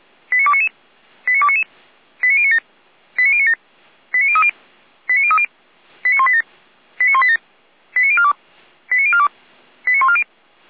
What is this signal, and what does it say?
Five-tone selective call (ZVEI), heard as quick runs of five tones, tones of about 70 ms. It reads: 89299 89299 89097 89097 89030 89030 89187 89187 89042 89042 89140
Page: SDR/ZVEI